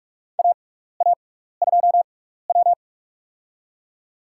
Morse code, heard as a keyed tone, about 47 wpm, spaced as for 16 wpm, 700 Hz. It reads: AA2W
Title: USA Call Signs - 4 Characters - 6x Character Spacing - 45wpm